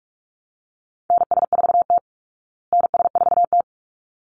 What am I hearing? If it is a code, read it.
Morse code, heard as a keyed tone, about 44 wpm, 700 Hz. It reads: NS4T NS4T
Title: Call Signs Repeated 3x - 45wpm